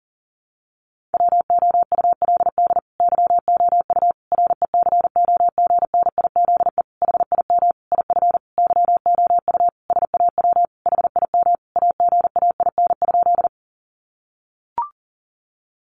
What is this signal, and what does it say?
Morse code, heard as a keyed tone, about 40 wpm, 700 Hz. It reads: WOULD YOU RECOGNIZE HIM IF YOU SAW HIM AGAIN?
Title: Sentences from Top 1400 Words 40wpm